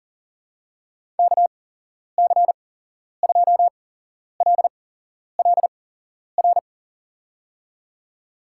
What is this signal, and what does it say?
Morse code, heard as a keyed tone, about 40 wpm, spaced as for 11 wpm, 700 Hz. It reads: KC2LLR
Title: USA Call Signs - 8x Character Spacing - 40wpm